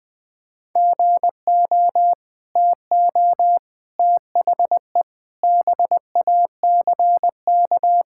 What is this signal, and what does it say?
Morse code, heard as a keyed tone, about 20 wpm, 700 Hz. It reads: GO TO THE BACK
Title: Sentences from Top 1000 Words - Review Part 5 - 20wpm